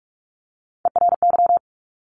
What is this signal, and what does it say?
Morse code, heard as a keyed tone, about 46 wpm, 700 Hz. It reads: ERY
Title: Top Three Letter Combinations in Words 45wpm